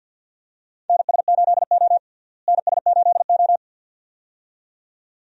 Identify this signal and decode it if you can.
Morse code, heard as a keyed tone, about 49 wpm, 700 Hz. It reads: NS8O NS8O
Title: USA Call Signs - 4 Characters - Repeated 2x - 50wpm